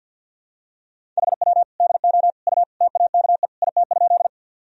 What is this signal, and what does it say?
Morse code, heard as a keyed tone, about 50 wpm, 700 Hz. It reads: HW DO U TAKE IT?
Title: Sentences from Top 200 Words and Abbreviations 50wpm